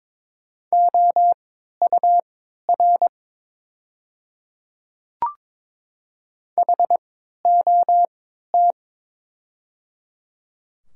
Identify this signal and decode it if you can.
Morse code, heard as a keyed tone, about 22 wpm, spaced as for 13 wpm, 700 Hz. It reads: OUR HOT
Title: 3x Character Spacing - Top 300 Words - 22wpm